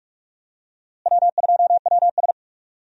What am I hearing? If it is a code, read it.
Morse code, heard as a keyed tone, about 41 wpm, 700 Hz. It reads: W2WS
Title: USA Call Signs - 4 Characters - Repeated 3x - 45wpm